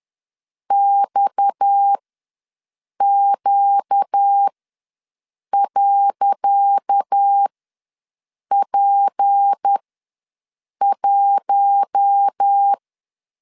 Morse code, heard as a keyed tone, about 10 wpm, spaced as for 6 wpm, 800 Hz. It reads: XQ.P1
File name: LEZIONI / 10 SIGNS / 10C DOT ws 10wpm 4213